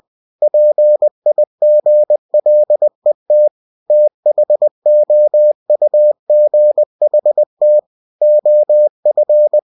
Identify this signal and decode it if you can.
Morse code, heard as a keyed tone, about 20 wpm, 600 Hz. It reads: PIGLET THOUGHT OF
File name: cw-books / winnie-the-pooh / winnie-20wpm-page-0070